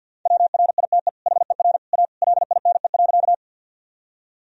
Morse code, heard as a keyed tone, about 50 wpm, 700 Hz. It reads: WRITE HER A LINE.
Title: Sentences from Top 1000 Words - Review Part 1 - 50wpm